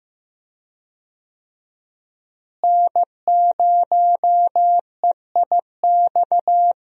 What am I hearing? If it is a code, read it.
Morse code, heard as a keyed tone, about 15 wpm, 700 Hz. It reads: N0EIX
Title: USA Call Signs - 5 Characters - Repeated 3x - 15wpm